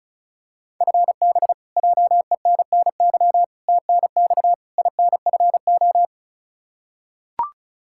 Morse code, heard as a keyed tone, about 36 wpm, 700 Hz. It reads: FB JENNY TNX INFO
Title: Practice QSO - Exchange 3 - More Info - 35wpm